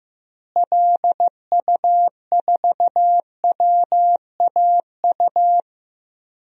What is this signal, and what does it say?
Morse code, heard as a keyed tone, about 15 wpm, 700 Hz. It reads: LU4WAU
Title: DXCC-100 Call Signs - 6 Characters - Repeated 3x - 15wpm